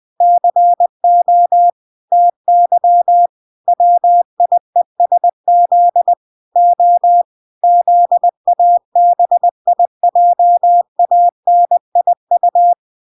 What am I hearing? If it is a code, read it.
Morse code, heard as a keyed tone, about 20 wpm, 700 Hz. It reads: CO TY WIESZ O ZABIJANIU